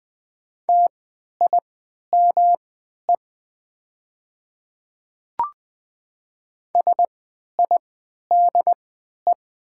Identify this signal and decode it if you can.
Morse code, heard as a keyed tone, about 20 wpm, spaced as for 11 wpm, 700 Hz. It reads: TIME SIDE
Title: 3x Character Spacing - Top 500 Words - 20wpm